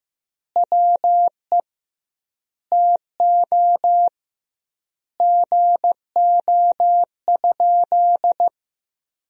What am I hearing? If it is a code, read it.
Morse code, heard as a keyed tone, about 15 wpm, 700 Hz. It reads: WE TO GO?
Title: 2x Word-Spacing - Sentences from Top 200 Words - 15wpm